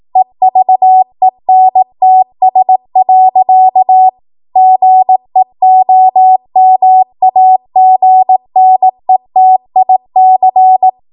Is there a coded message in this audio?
Yes: Morse code from 0.2 to 10.9 s, heard as a keyed tone, about 18 wpm, 750 Hz. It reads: EVENTS. GEOMAGNETIC